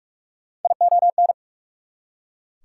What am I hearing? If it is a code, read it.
Morse code, heard as a keyed tone, about 44 wpm, 690 Hz. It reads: ION
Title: Prefixes and Suffixes 45wpm